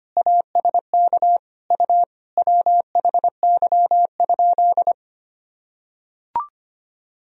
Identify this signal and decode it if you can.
Morse code, heard as a keyed tone, about 25 wpm, 700 Hz. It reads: ASK U WHY?
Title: Sentences from Top 200 Words and Abbreviations 25wpm